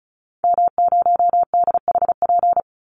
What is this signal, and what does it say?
Morse code, heard as a keyed tone, about 34 wpm, 700 Hz. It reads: M0DHP
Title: Call Signs - 35wpm